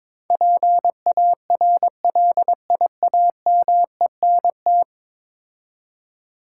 Morse code, heard as a keyed tone, about 21 wpm, 700 Hz. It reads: PARLIAMENT
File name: practice / 22wpm / 10 Letter Words - First 3 Letters Given - 22wpm